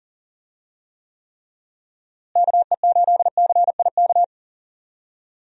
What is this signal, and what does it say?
Morse code, heard as a keyed tone, about 39 wpm, 700 Hz. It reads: KE8CIK
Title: USA Call Signs - 6 Characters - Repeated 3x - 40wpm